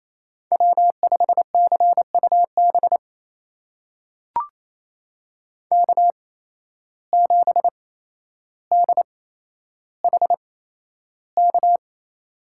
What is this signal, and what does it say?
Morse code, heard as a keyed tone, about 28 wpm, spaced as for 12 wpm, 700 Hz. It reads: W5CUB K7DHK
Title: USA Call Signs - 5 Characters - 8x Character Spacing - 28wpm